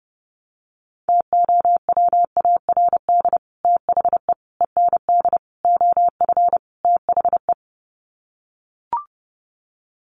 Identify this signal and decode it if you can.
Morse code, heard as a keyed tone, about 30 wpm, 700 Hz. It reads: TOWARD THE END OF THE
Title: Sets of 5 Words 30wpm